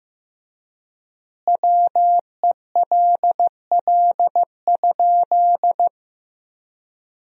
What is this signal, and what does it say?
Morse code, heard as a keyed tone, about 15 wpm, 700 Hz. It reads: WELL?
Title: 3x Word-Spacing - Sentences from Top 200 Words - 15wpm